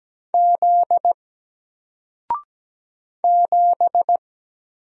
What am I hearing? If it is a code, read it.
Morse code, heard as a keyed tone, about 17 wpm, 700 Hz. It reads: Z 7
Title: Single Character (Letter:Number:Punctuation:Prosign) Alt 17wpm